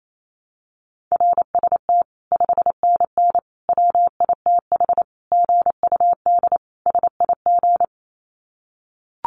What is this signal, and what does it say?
Morse code, heard as a keyed tone, about 28 wpm, 700 Hz. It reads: RST 5NN WITH GUD SIG
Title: Common QSO Phrases 28wpm